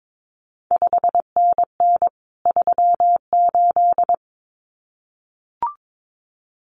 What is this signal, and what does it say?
Morse code, heard as a keyed tone, about 22 wpm, 700 Hz. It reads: 5NN 38
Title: CQ WW DX Contest Exchange Encore - 22wpm